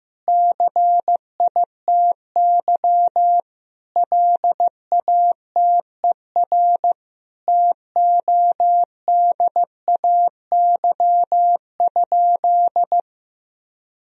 Morse code, heard as a keyed tone, about 16 wpm, 700 Hz. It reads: CITY LATER TODAY?
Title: Sentences from Top 1000 Words - Review Part 2 - 15wpm